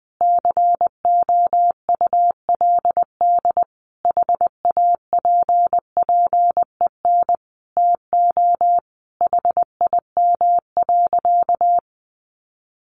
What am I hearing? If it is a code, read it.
Morse code, heard as a keyed tone, about 20 wpm, 700 Hz. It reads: COULD HAPPEN TO HIM.
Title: Sentences from Top 400 Words 20wpm